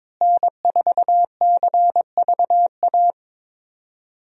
Morse code, heard as a keyed tone, about 22 wpm, 700 Hz. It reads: N4CVA
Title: USA Call Signs - 5 Characters - Repeated 3x - 22wpm